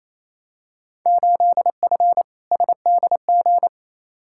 Morse code, heard as a keyed tone, about 28 wpm, 700 Hz. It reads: 8F SDG
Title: Field Day Exchanges Encore - 28wpm